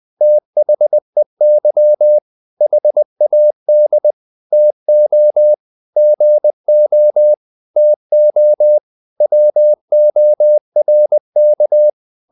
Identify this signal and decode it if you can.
Morse code, heard as a keyed tone, about 20 wpm, 600 Hz. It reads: THEY HAD TO GO TO WORK
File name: the old man and the sea part 2 20wpm